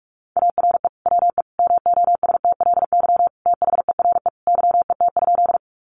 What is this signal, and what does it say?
Morse code, heard as a keyed tone, about 46 wpm, 700 Hz. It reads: ARE WE MOSTLY THERE YET?
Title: Sentences from Top 1300 Words 45wpm